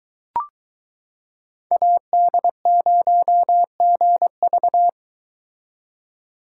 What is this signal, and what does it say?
Morse code, heard as a keyed tone, about 23 wpm, 700 Hz. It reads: AD0GV